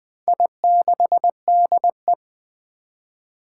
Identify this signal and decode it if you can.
Morse code, heard as a keyed tone, about 20 wpm, 700 Hz. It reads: I6DE
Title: Four Challenging Letter-Number Combinations 20wpm